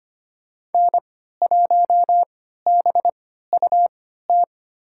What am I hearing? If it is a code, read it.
Morse code, heard as a keyed tone, about 25 wpm, spaced as for 14 wpm, 700 Hz. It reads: N1BUT
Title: USA Call Signs - 3x Character Spacing - 25wpm